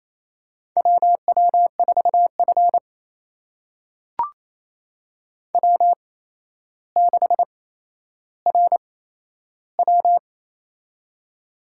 Morse code, heard as a keyed tone, about 28 wpm, spaced as for 12 wpm, 700 Hz. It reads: WW4F W6RW
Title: USA Call Signs - 4 Characters - 8x Character Spacing - 28wpm